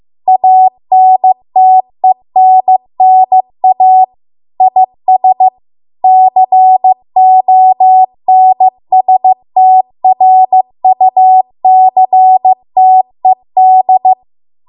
Morse code, heard as a keyed tone, about 15 wpm, 750 Hz. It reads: ANTENNA IS CONSTRUCTED